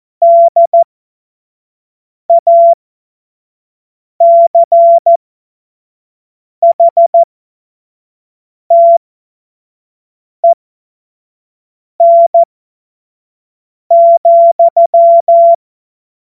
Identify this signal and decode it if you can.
Morse code, heard as a keyed tone, about 14 wpm, spaced as for 5 wpm, 680 Hz. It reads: DACHTEN,
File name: Morsekurs / Witze / Witz 15. November 2022 25 F70